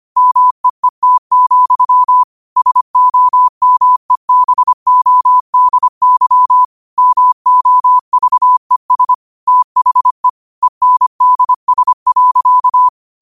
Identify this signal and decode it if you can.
Morse code, heard as a keyed tone, about 25 wpm, 1000 Hz. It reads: MEET, SOMEBODY MOVES THE ENDS.
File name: qotdfiles / 25wpm